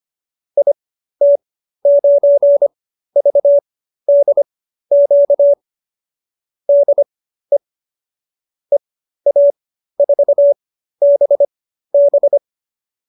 Morse code, heard as a keyed tone, about 25 wpm, spaced as for 13 wpm, 570 Hz. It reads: IT9VDQ DE EA4BB
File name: qso215b 13